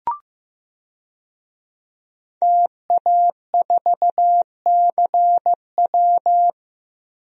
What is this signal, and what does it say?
Morse code, heard as a keyed tone, about 15 wpm, 700 Hz. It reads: TA4CW
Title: DXCC-100 Call Signs - 5 Characters - Repeated 2x - 15wpm